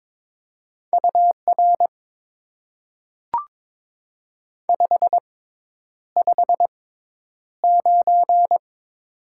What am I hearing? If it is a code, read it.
Morse code, heard as a keyed tone, about 22 wpm, spaced as for 10 wpm, 700 Hz. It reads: UR 559
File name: Top 90 QSO Elements - 6x Character Spacing - 22wpm